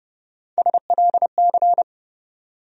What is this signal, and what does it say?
Morse code, heard as a keyed tone, about 30 wpm, 700 Hz. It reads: SLC
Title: USA Call Sign Postfixes - 3 Letters 30wpm